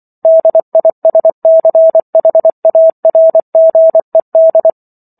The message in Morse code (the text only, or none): DISCHARGED